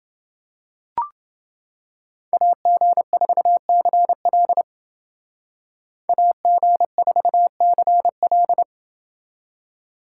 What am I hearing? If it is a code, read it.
Morse code, heard as a keyed tone, about 28 wpm, 700 Hz. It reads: AG4CL AG4CL